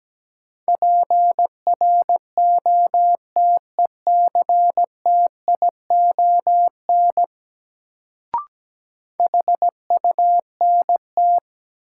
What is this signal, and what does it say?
Morse code, heard as a keyed tone, about 17 wpm, 700 Hz. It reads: PROTECTION HUNT
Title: Sentences from Top 1600 Words - New Word Drill - 17wpm